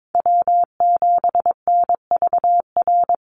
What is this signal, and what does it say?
Morse code, heard as a keyed tone, about 22 wpm, 700 Hz. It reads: W7NVR
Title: USA Call Signs 22wpm